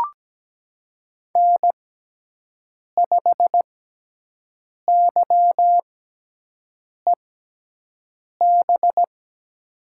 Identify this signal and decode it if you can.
Morse code, heard as a keyed tone, about 17 wpm, spaced as for 6 wpm, 700 Hz. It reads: N5YEB
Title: USA Call Signs - 5 Characters - 6x Character Spacing - 17wpm